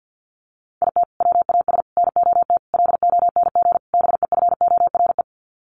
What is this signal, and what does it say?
Morse code, heard as a keyed tone, about 50 wpm, 700 Hz. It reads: IT WAS NOT LONG BEFORE
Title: Sets of 5 Words Encore2 50wpm